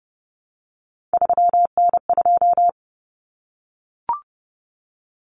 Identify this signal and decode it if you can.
Morse code, heard as a keyed tone, about 29 wpm, 700 Hz. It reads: 3N2
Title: Common Cut Numbers - 3-Digit 30wpm